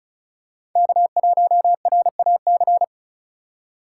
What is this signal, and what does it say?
Morse code, heard as a keyed tone, about 36 wpm, 700 Hz. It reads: K1RAC
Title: USA Call Signs - 5 Characters - 2x Character Spacing - 35wpm